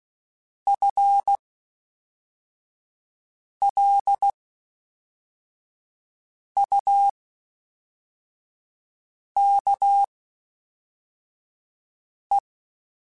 Morse code, heard as a keyed tone, about 16 wpm, spaced as for 4 wpm, 800 Hz. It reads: FLUKE
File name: Spaced / FL Runs sp mp3s / FL Runs sp @5